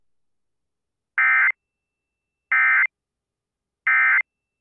Emergency Alert System (SAME) tones, heard as harsh buzzing bursts. It is heard from 1.2 to 4.2 s.